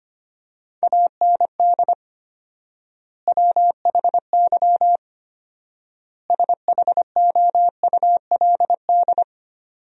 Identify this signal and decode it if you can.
Morse code, heard as a keyed tone, about 25 wpm, 700 Hz. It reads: AND WHY SHOULD
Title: 4x Word-Spacing - Sentences from Top 300 Words - 25wpm